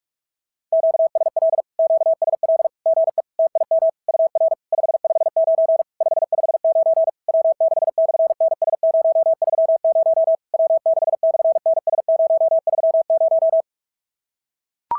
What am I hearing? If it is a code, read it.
Morse code, heard as a keyed tone, about 44 wpm, 650 Hz. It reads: QSL QSL GE TIM UR 559 559 W6/NS030 W6/NS030
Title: SOTA QSO Practice 45wpm